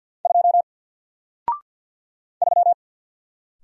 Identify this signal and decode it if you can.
Morse code, heard as a keyed tone, about 48 wpm, 700 Hz. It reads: 2 3
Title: Single Letter-Number Alt 50wpm